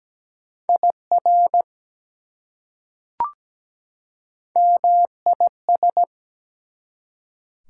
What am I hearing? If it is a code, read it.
Morse code, heard as a keyed tone, about 17 wpm, 700 Hz. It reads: IR MIS
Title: Prefixes and Suffixes 17wpm